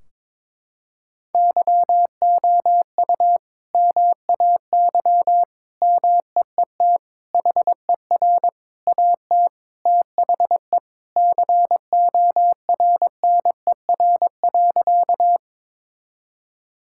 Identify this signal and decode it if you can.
Morse code, heard as a keyed tone, about 21 wpm, 700 Hz. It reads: YOU MAY MEET HER AT THE CORNER.
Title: Sentences from Top 1600 Words 22wpm